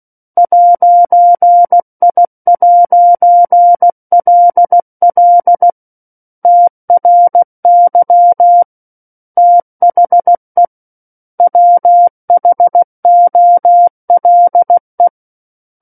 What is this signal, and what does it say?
Morse code, heard as a keyed tone, about 16 wpm, 700 Hz. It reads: 'I'LL TRY THE WHOLE